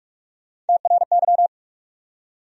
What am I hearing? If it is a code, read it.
Morse code, heard as a keyed tone, about 58 wpm, 700 Hz. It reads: TRY